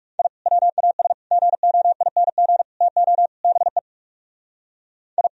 Morse code, heard as a keyed tone, about 36 wpm, 700 Hz. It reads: I WAS GOING TO BE I
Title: Speed-Racing - Sets of 5 Words 30wpm